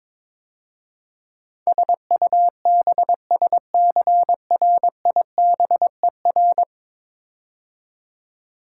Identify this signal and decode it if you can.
Morse code, heard as a keyed tone, about 22 wpm, 700 Hz. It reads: SUBSCRIBER
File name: practice / 22wpm / 10 Letter Words - First 3 Letters Given - 22wpm